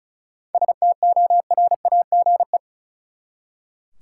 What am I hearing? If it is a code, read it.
Morse code, heard as a keyed tone, about 34 wpm, 700 Hz. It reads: STORAGE
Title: Seven Letter Words - 35wpm